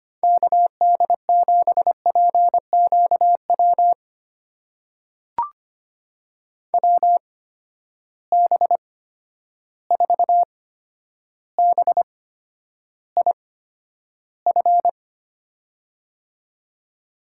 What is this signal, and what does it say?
Morse code, heard as a keyed tone, about 25 wpm, spaced as for 11 wpm, 700 Hz. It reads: KD7PQW WB4BIF
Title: USA Call Signs - 6 Characters - 8x Character Spacing - 25wpm